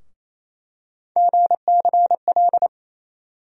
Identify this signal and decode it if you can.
Morse code, heard as a keyed tone, about 28 wpm, 700 Hz. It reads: GCL
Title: USA Call Sign Postfixes - 3 Letters 28wpm